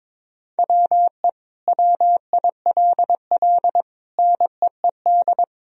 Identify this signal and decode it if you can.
Morse code, heard as a keyed tone, about 22 wpm, 700 Hz. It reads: WE WILL NEED
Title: Sentences from Top 1000 Words - Review Part 4 - 22wpm